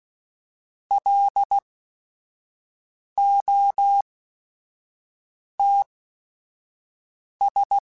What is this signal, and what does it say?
Morse code, heard as a keyed tone, about 16 wpm, spaced as for 5 wpm, 800 Hz. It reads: LOTS
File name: Weather Sets / Weather Sets-mp3s / Weather 5 12 @5